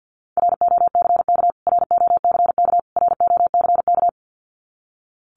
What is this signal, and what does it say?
Morse code, heard as a keyed tone, about 48 wpm, 700 Hz. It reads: ROCK ROCK ROCK
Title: Top 500 Words Repeated 50wpm